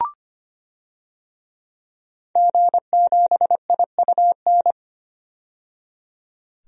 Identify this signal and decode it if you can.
Morse code, heard as a keyed tone, about 25 wpm, 700 Hz. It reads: G7IUN
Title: DXCC-100 Call Signs - 5 Characters - 25wpm